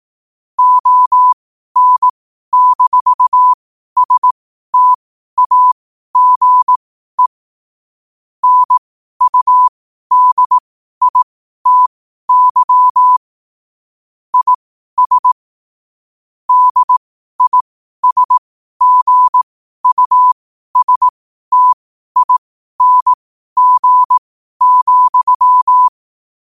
Morse code, heard as a keyed tone, about 18 wpm, spaced as for 12 wpm, 1000 Hz. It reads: ON-STAGE NUDITY IS DISGUSTING,
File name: qotdfiles / 12wpm